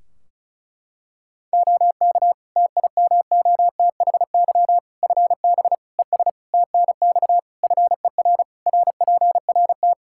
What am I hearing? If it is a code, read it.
Morse code, heard as a keyed tone, about 34 wpm, 700 Hz. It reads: OK TIMOTHY FB ES TNX FER RPRT
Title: Practice QSO - Exchange 2 - Info - 35wpm